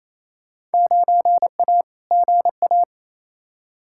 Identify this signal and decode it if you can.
Morse code, heard as a keyed tone, about 28 wpm, 700 Hz. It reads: 9A GA
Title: Field Day Exchanges Encore - 28wpm